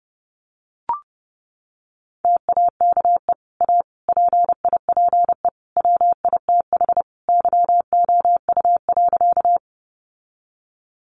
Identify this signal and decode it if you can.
Morse code, heard as a keyed tone, about 30 wpm, 700 Hz. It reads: TAKE A PIPE WITH YOU.
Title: Sentences from Top 1500 Words 30wpm